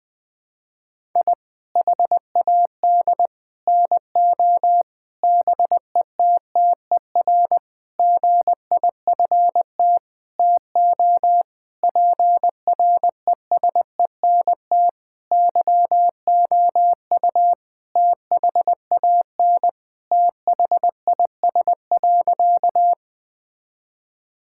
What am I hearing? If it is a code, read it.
Morse code, heard as a keyed tone, about 20 wpm, 700 Hz. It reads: I HAD NO BETTER GIFT TO PRESENT YOU THAN THIS.
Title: Sentences from Top 1000 Words - Review Part 6 - 20wpm